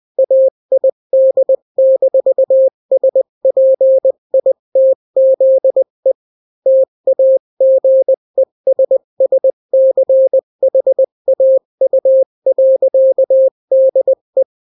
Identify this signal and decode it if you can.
Morse code, heard as a keyed tone, about 20 wpm, 530 Hz. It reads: AID-SPITZE TAGESSCHAU.DE
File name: daily / 18wpm